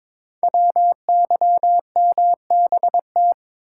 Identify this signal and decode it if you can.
Morse code, heard as a keyed tone, about 22 wpm, 700 Hz. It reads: WYMBT
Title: Code Groups - 5 Letters - 22wpm